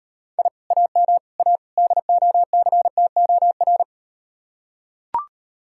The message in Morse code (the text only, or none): I AM A DOCTOR